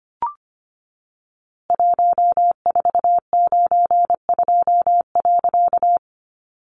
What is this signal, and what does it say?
Morse code, heard as a keyed tone, about 25 wpm, 700 Hz. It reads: 1492.
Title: A Sound Of Thunder 25wpm